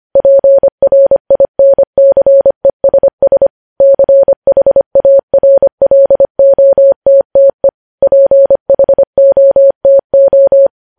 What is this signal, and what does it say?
Morse code, heard as a keyed tone, about 25 wpm, 550 Hz. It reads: PRINCESS CHARLOTTE PHOTO
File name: podcast / News headlines-1x-25x25